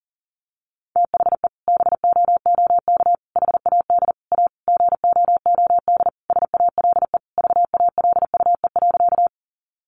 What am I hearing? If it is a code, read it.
Morse code, heard as a keyed tone, about 40 wpm, 700 Hz. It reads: THE BOOK HAD A GOOD SALE VALUE.
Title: Sentences from Top 1400 Words 40wpm